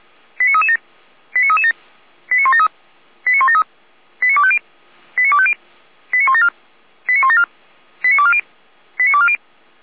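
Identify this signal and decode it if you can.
Five-tone selective call (ZVEI), heard as quick runs of five tones, tones of about 70 ms. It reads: 89337 89337 89173 89173 89250 89250 89174 89174 89240 89240
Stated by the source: ZVEI five-tone call 89160 received on SDR, archived by feuerwehr-orlamuende.org fire brigade.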